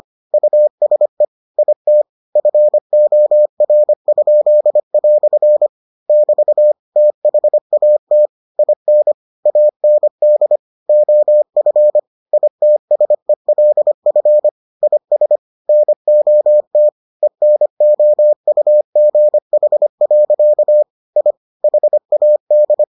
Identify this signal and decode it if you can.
Morse code, heard as a keyed tone, about 25 wpm, 600 Hz. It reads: USE IT FOR?" <BT> THAT IN AND OF ITSELF IS NOT ENOUGH. I HAD